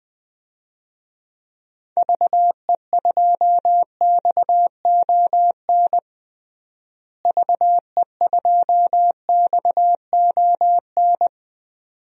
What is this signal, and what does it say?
Morse code, heard as a keyed tone, about 20 wpm, 700 Hz. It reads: VE2XON VE2XON
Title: DXCC-100 Call Signs - 6 Characters - Repeated 2x - 20wpm